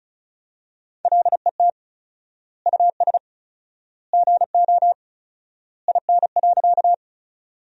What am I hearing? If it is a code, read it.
Morse code, heard as a keyed tone, about 34 wpm, 700 Hz. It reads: LET US GO IN.
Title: 4x Word-Spacing - Sentences from Top 300 Words - 35wpm